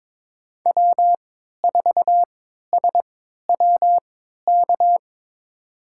Morse code, heard as a keyed tone, about 21 wpm, spaced as for 12 wpm, 700 Hz. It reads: W4SWK
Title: USA Call Signs - 3x Character Spacing - 22wpm